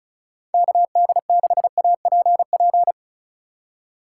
Morse code, heard as a keyed tone, about 36 wpm, 690 Hz. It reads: KD6APP
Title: USA Call Signs - 6 Characters - 2x Character Spacing - 35wpm